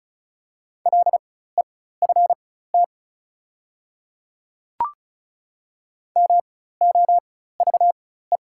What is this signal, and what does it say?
Morse code, heard as a keyed tone, about 35 wpm, spaced as for 16 wpm, 690 Hz. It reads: LEFT MOVE
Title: 4x Character Spacing - Top 500 Words - 35wpm